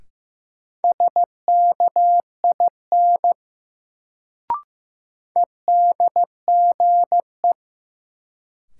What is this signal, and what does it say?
Morse code, heard as a keyed tone, about 15 wpm, 700 Hz. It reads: SKIN EDGE
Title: Sentences from Top 1300 Words - New Word Drill Alt - 15wpm